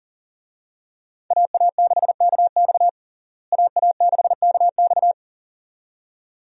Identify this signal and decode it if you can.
Morse code, heard as a keyed tone, about 40 wpm, 700 Hz. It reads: AA6KX AA6KX
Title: USA Call Signs - 5 Characters - Repeated 2x - 40wpm